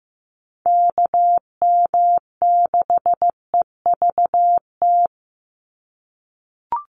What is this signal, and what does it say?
Morse code, heard as a keyed tone, about 15 wpm, 700 Hz. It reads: KM6EVT
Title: USA Call Signs 15wpm